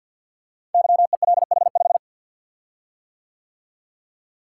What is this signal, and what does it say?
Morse code, heard as a keyed tone, about 48 wpm, 700 Hz. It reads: YELH5
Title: Code Groups - 5 Letter:Numbers - 50wpm